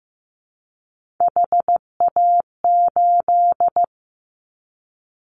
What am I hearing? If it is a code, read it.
Morse code, heard as a keyed tone, about 15 wpm, 700 Hz. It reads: HA8
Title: Callsign Prefixes for Top 50 DXCC - 15wpm